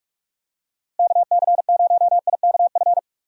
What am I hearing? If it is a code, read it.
Morse code, heard as a keyed tone, about 41 wpm, 700 Hz. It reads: KC0IKF
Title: USA Call Signs - 6 Characters - Repeated 3x - 45wpm